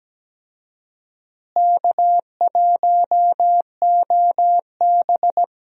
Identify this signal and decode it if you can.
Morse code, heard as a keyed tone, about 17 wpm, 700 Hz. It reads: K1OB
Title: USA Call Signs - 4 Characters - Repeated 2x - 17wpm